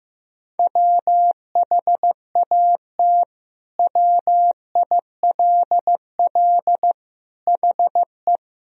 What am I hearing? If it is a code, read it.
Morse code, heard as a keyed tone, about 15 wpm, 700 Hz. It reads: WHAT WILL HE
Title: Sentences from Top 1000 Words - Review Part 3 - 15wpm